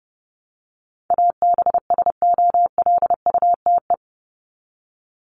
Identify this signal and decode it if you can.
Morse code, heard as a keyed tone, about 30 wpm, 700 Hz. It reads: ABSOLUTE
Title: LY Suffix - Top 100 Words with Base Words 30wpm